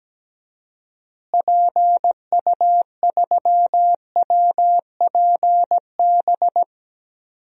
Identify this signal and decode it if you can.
Morse code, heard as a keyed tone, about 17 wpm, 700 Hz. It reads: PU3WPB
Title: DXCC-100 Call Signs - 6 Characters - Repeated 3x - 17wpm